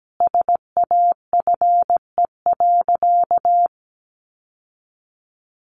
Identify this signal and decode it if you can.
Morse code, heard as a keyed tone, about 17 wpm, 700 Hz. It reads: SAFE.
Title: Sentences from Top 900 Words 17wpm